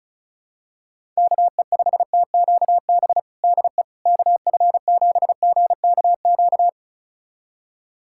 Morse code, heard as a keyed tone, about 34 wpm, 700 Hz. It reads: KE5TQB DE KF7GKQ
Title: USA Callsign DE Callsign - 6 Characters - 35wpm